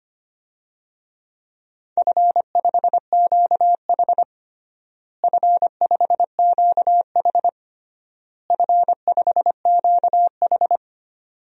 Morse code, heard as a keyed tone, about 25 wpm, 700 Hz. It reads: F5QH F5QH F5QH
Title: DXCC-100 Call Signs - 4 Characters - Repeated 3x - 25wpm